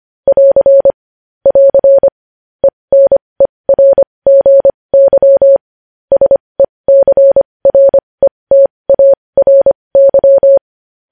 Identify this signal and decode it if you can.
Morse code, heard as a keyed tone, about 25 wpm, spaced as for 20 wpm, 550 Hz. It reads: <AR> <AR> ENERGY SECRETARY
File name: podcast / News headlines-1x-25x20